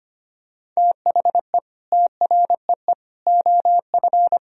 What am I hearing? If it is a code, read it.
Morse code, heard as a keyed tone, about 25 wpm, 700 Hz. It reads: THE TREE OF
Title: Sentences from Top 1000 Words - Review Part 3 - 25wpm